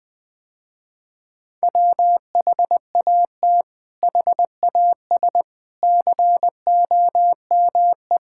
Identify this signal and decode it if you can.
Morse code, heard as a keyed tone, about 19 wpm, 700 Hz. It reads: WHAT HAS COME
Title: Sentences with 8-Word Phrases - Spoken Beginning - 20wpm